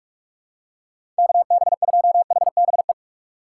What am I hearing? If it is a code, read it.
Morse code, heard as a keyed tone, about 44 wpm, 690 Hz. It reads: KB2HBE
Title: USA Call Signs - 6 Characters - 3x Character Spacing - 45wpm